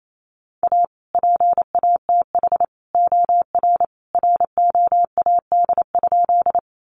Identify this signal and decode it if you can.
Morse code, heard as a keyed tone, about 28 wpm, 700 Hz. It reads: A PATH OR ROAD?
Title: Sentences from Top 1400 Words 28wpm